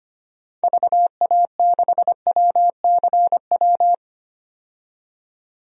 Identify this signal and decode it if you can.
Morse code, heard as a keyed tone, about 25 wpm, 700 Hz. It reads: VA6WCW
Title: DXCC-100 Call Signs - 6 Characters - 25wpm